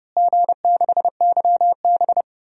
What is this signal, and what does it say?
Morse code, heard as a keyed tone, about 30 wpm, 700 Hz. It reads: G6YB
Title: DXCC-100 Call Signs - 4 Characters - Repeated 3x - 30wpm